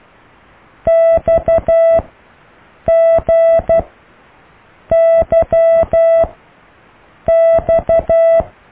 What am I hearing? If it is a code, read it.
Morse code, heard as a keyed tone, about 12 wpm, spaced as for 6 wpm, 660 Hz. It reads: XGYX